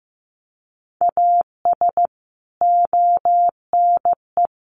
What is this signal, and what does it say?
Morse code, heard as a keyed tone, about 15 wpm, 700 Hz. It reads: AS ONE
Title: Sets of 5 Words 15wpm